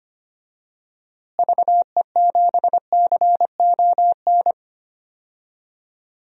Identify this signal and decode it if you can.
Morse code, heard as a keyed tone, about 25 wpm, 700 Hz. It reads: VE7CON
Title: DXCC-100 Call Signs - 6 Characters - Repeated 2x - 25wpm